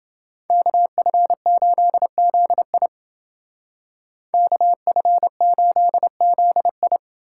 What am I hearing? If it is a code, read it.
Morse code, heard as a keyed tone, about 28 wpm, 700 Hz. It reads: KF8ZI KF8ZI